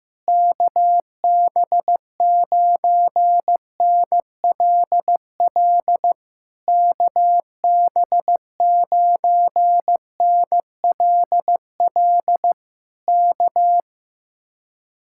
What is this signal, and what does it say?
Morse code, heard as a keyed tone, about 15 wpm, 700 Hz. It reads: KB9NLL KB9NLL K
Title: Speed-Racing Common QSO Phrases Encore - 15wpm